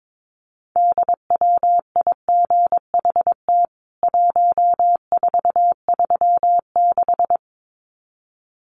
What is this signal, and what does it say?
Morse code, heard as a keyed tone, about 22 wpm, 700 Hz. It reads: DWIGHT 1436
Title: CWT Exchange 22wpm